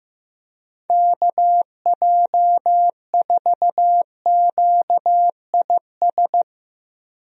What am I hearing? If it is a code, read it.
Morse code, heard as a keyed tone, about 15 wpm, 700 Hz. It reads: KJ4QIS
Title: USA Call Signs - 6 Characters - 15wpm